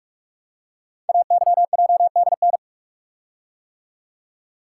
Morse code, heard as a keyed tone, about 44 wpm, 700 Hz. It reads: AYJDN
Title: Code Groups - 5 Letters - 45wpm